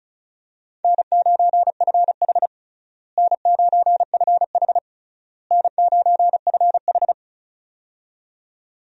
Morse code, heard as a keyed tone, about 36 wpm, 700 Hz. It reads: N9FH N9FH N9FH
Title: USA Call Signs - 4 Characters - Repeated 3x - 35wpm